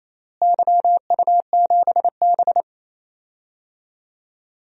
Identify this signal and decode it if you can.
Morse code, heard as a keyed tone, about 28 wpm, 700 Hz. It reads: YU7B
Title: DXCC-100 Call Signs - 4 Characters - Repeated 2x - 28wpm